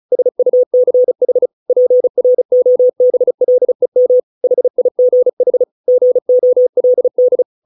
Morse code, heard as a keyed tone, about 34 wpm, 500 Hz. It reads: SUCH PROBLEM HIGH GOLD